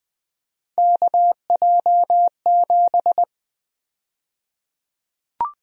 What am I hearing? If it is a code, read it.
Morse code, heard as a keyed tone, about 21 wpm, 700 Hz. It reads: KJ7
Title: USA Call Sign Prefixes - Repeated 2x - 20wpm